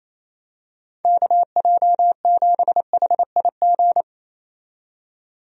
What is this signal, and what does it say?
Morse code, heard as a keyed tone, about 28 wpm, 700 Hz. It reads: KJ7HIG
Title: USA Call Signs - 6 Characters - 28wpm